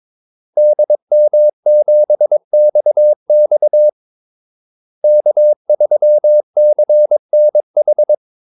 Morse code, heard as a keyed tone, about 22 wpm, 600 Hz. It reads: DM7XX K3CNH
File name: Callsigns-22wpm